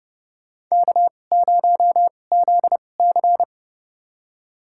Morse code, heard as a keyed tone, about 31 wpm, spaced as for 22 wpm, 700 Hz. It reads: K0ZC
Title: USA Call Signs - 4 Characters - 2x Character Spacing - 30wpm